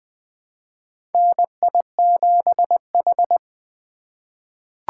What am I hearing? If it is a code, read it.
Morse code, heard as a keyed tone, about 20 wpm, 700 Hz. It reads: NI7H
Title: USA Call Signs - 4 Characters - 4x Character Spacing - 20wpm